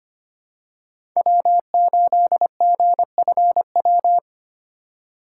Morse code, heard as a keyed tone, about 25 wpm, 700 Hz. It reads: W8GFW
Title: USA Call Signs - 5 Characters - 3x Character Spacing - 25wpm